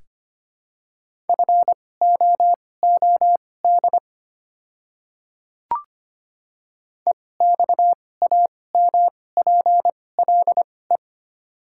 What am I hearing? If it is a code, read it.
Morse code, heard as a keyed tone, about 25 wpm, spaced as for 18 wpm, 700 Hz. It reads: FOOD EXAMPLE
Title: 2x Character Spacing - Top 300 Words - 25wpm